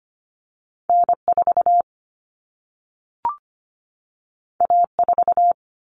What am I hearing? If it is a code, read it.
Morse code, heard as a keyed tone, about 25 wpm, 700 Hz. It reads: N4 A4
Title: Common Cut Numbers - 2-Digit 25wpm